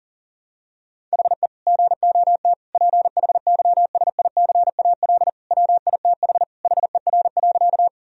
Morse code, heard as a keyed tone, about 40 wpm, 700 Hz. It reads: HE GOT PHYSICAL WITH HER.
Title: Sentences from Top 1000 Words - Review Part 4 - 40wpm